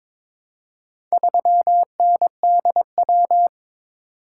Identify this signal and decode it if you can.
Morse code, heard as a keyed tone, about 23 wpm, 700 Hz. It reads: 3NDW
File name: Four Letter-Number Combinations - 22wpm